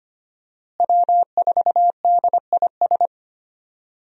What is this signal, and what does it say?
Morse code, heard as a keyed tone, about 25 wpm, 700 Hz. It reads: W4DIS
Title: USA Call Signs - 5 Characters - Repeated 3x - 25wpm